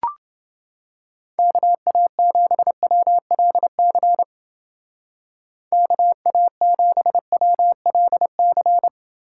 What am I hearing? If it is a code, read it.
Morse code, heard as a keyed tone, about 30 wpm, 700 Hz. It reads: KA7WLC KA7WLC